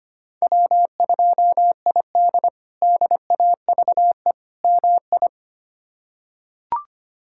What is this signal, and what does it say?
Morse code, heard as a keyed tone, about 25 wpm, 700 Hz. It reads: W2ID DAVE MI
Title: K1USN SST QSO Practice - 25wpm